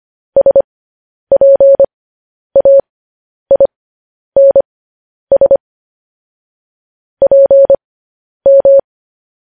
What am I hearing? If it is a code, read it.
Morse code, heard as a keyed tone, about 25 wpm, spaced as for 10 wpm, 550 Hz. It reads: SPAINS PM